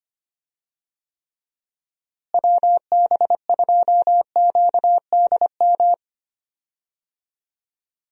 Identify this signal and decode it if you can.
Morse code, heard as a keyed tone, about 25 wpm, 700 Hz. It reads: WB2QDM
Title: USA Call Signs - 6 Characters - Repeated 3x - 25wpm